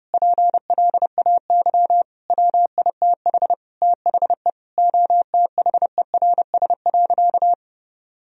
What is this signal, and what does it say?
Morse code, heard as a keyed tone, about 30 wpm, 700 Hz. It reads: PLAY WITH THE OTHERS.
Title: Sentences from Top 1000 Words - Review Part 2 - 30wpm